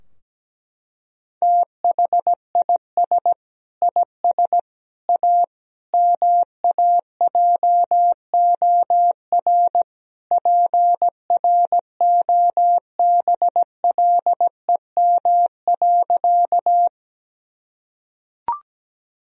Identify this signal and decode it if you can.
Morse code, heard as a keyed tone, about 17 wpm, 700 Hz. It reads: THIS IS A MAJOR PROBLEM.
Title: Sentences from Top 1000 Words - Review Part 4 - 17wpm